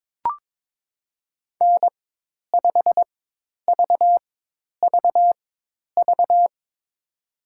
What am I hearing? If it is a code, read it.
Morse code, heard as a keyed tone, about 22 wpm, spaced as for 10 wpm, 700 Hz. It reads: N5VVV
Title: USA Call Signs - 5 Characters - 4x Character Spacing - 22wpm